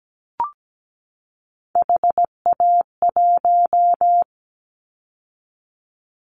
Morse code, heard as a keyed tone, about 17 wpm, 700 Hz. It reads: HA1
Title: Callsign Prefixes for Top 50 DXCC - 17wpm